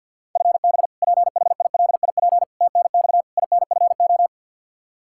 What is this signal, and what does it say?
Morse code, heard as a keyed tone, about 47 wpm, 700 Hz. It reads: FB PHILIP TNX INFO
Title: Practice QSO - Exchange 3 - More Info - 50wpm